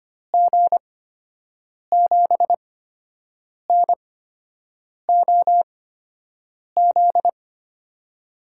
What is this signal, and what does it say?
Morse code, heard as a keyed tone, about 25 wpm, spaced as for 7 wpm, 700 Hz. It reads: G7NOZ
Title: USA Call Signs - 6 Characters - 8x Character Spacing - 25wpm